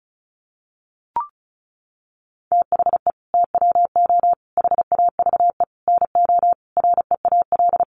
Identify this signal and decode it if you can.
Morse code, heard as a keyed tone, about 34 wpm, 700 Hz. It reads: THE TWO HAVE NO REAL
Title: Sentences from Top 1300 Words 35wpm